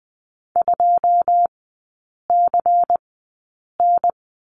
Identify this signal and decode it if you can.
Morse code, heard as a keyed tone, about 20 wpm, spaced as for 8 wpm, 700 Hz. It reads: 2CN
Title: 121 Lesson 20e - Number 3 - Three Characters 20wpm